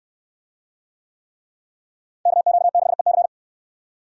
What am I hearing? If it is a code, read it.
Morse code, heard as a keyed tone, about 69 wpm, 700 Hz. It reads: KQ6EC